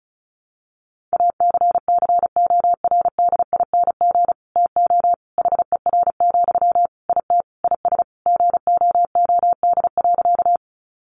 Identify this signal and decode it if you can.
Morse code, heard as a keyed tone, about 34 wpm, 700 Hz. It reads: ACCORDING TO HER, IT IS GOOD.
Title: Sentences from Top 800 Words 35wpm